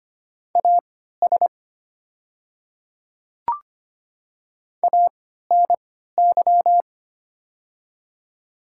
Morse code, heard as a keyed tone, about 25 wpm, spaced as for 14 wpm, 700 Hz. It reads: AS ANY
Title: 3x Character Spacing - Top 200 Words - 25wpm